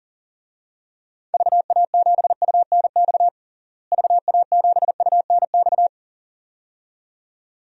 Morse code, heard as a keyed tone, about 40 wpm, 700 Hz. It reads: VA7UNX VA7UNX
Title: DXCC-100 Call Signs - 6 Characters - Repeated 2x - 40wpm